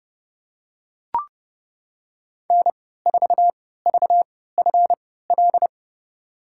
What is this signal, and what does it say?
Morse code, heard as a keyed tone, about 32 wpm, spaced as for 17 wpm, 700 Hz. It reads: N4VFL
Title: USA Call Signs - 5 Characters - 3x Character Spacing - 30wpm